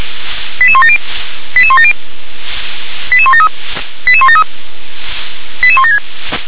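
Five-tone selective call (ZVEI), heard as quick runs of five tones, tones of about 70 ms. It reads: 88170 88170 88173 88173 88176
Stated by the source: ZVEI five-tone selective call received by SDR.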